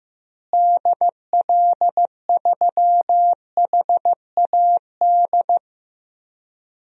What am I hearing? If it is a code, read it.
Morse code, heard as a keyed tone, about 15 wpm, 700 Hz. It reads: DL3HAD
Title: DXCC-100 Call Signs - 6 Characters - Repeated 2x - 15wpm